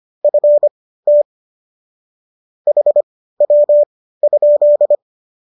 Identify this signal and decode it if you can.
Morse code, heard as a keyed tone, about 25 wpm, spaced as for 15 wpm, 570 Hz. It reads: FT HW?